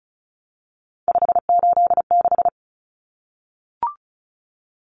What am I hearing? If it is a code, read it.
Morse code, heard as a keyed tone, about 34 wpm, 700 Hz. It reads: 586
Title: Common Cut Numbers - 3-Digit 35wpm